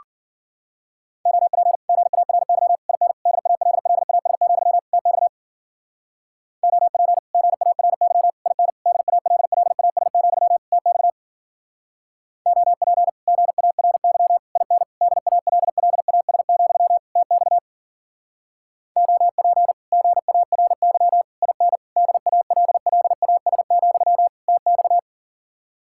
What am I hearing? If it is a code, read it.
Morse code, heard as a keyed tone, about 52 wpm, 700 Hz. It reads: OP GARY IN DALLAS, TX OP GARY IN DALLAS, TX OP GARY IN DALLAS, TX OP GARY IN DALLAS, TX